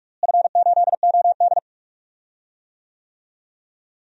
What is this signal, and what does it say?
Morse code, heard as a keyed tone, about 42 wpm, 700 Hz. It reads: F8OD
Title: DXCC-100 Call Signs - 4 Characters - Repeated 2x - 45wpm